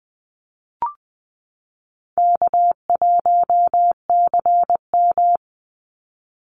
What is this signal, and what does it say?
Morse code, heard as a keyed tone, about 20 wpm, 700 Hz. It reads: K1CM
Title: NA SOTA Chaser Callsigns 20wpm